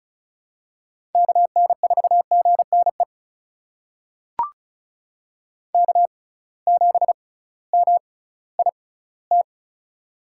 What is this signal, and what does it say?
Morse code, heard as a keyed tone, about 36 wpm, spaced as for 19 wpm, 700 Hz. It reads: KN4GNE K7MIT
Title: USA Call Signs - 6x Character Spacing - 35wpm